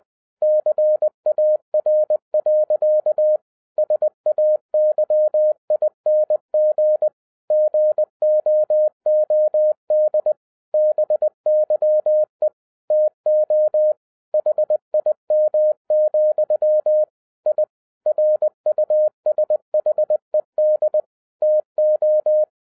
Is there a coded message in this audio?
Yes: Morse code from 0.4 to 22.4 s, heard as a keyed tone, about 20 wpm, 610 Hz. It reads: CAR. SAYING GOOD BYE TO HIM, I RUSHED TO